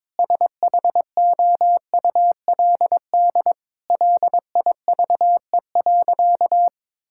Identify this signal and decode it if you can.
Morse code, heard as a keyed tone, about 22 wpm, 700 Hz. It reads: SHOULD LIVE.
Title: Sentences from Top 1000 Words - Review Part 5 - 22wpm